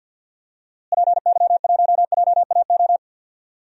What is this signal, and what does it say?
Morse code, heard as a keyed tone, about 52 wpm, 700 Hz. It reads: PY1JAO
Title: DXCC-100 Call Signs - 6 Characters - Repeated 2x - 50wpm